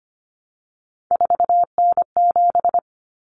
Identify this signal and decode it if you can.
Morse code, heard as a keyed tone, about 25 wpm, 700 Hz. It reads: 4N7
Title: Common Cut Numbers - 3-Digit 25wpm